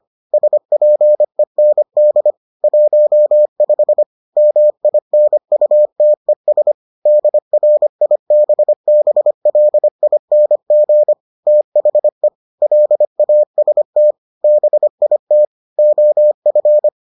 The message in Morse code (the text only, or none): SPEND 15 MINUTES DRIBBLING THE LAST BIT OF